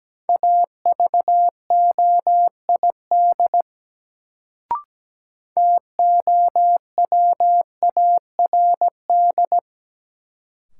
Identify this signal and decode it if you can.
Morse code, heard as a keyed tone, about 17 wpm, 700 Hz. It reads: AVOID TOWARD
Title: Sentences from Top 1000 Words - Review Part 6 - New Word Drill - 17wpm